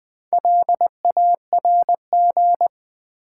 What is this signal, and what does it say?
Morse code, heard as a keyed tone, about 20 wpm, 700 Hz. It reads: LARG
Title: Build A Word - 1 Letter At A Time - 3 to 5-Letter Words - 20wpm